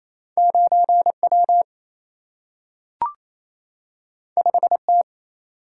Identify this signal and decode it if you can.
Morse code, heard as a keyed tone, about 28 wpm, 700 Hz. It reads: 9W 5T
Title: Single-Letter Single-Number 28wpm